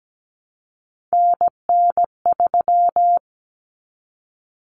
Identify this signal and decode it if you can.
Morse code, heard as a keyed tone, about 17 wpm, 700 Hz. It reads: NN3
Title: Common Cut Numbers - 3-Digit 17wpm